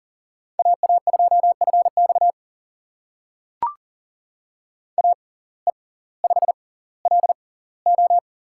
Morse code, heard as a keyed tone, about 40 wpm, spaced as for 20 wpm, 700 Hz. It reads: AA2FX AE5LO
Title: USA Call Signs - 5 Characters - 6x Character Spacing - 40wpm